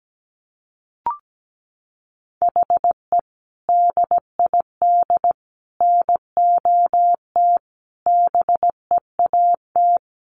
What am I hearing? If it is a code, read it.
Morse code, heard as a keyed tone, about 17 wpm, 700 Hz. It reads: HE DID NOT BEAT
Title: Sentences from Top 1300 Words 17wpm